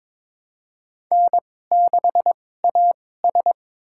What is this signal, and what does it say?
Morse code, heard as a keyed tone, about 23 wpm, spaced as for 16 wpm, 700 Hz. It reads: N6AS
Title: USA Call Signs - 4 Characters - 2x Character Spacing - 22wpm